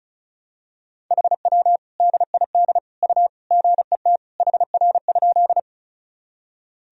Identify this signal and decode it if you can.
Morse code, heard as a keyed tone, about 36 wpm, 700 Hz. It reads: HW DID U GET HR?
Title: Sentences from Top 200 Words and Abbreviations 35wpm